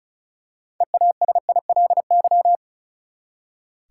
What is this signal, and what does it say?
Morse code, heard as a keyed tone, about 36 wpm, 700 Hz. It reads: EASILY
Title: Six Letter Words - 35wpm